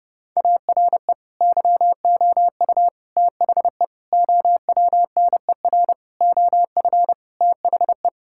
Morse code, heard as a keyed tone, about 30 wpm, 700 Hz. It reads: ARE YOU THE OWNER OF THE
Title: Sentences from Top 1600 Words 30wpm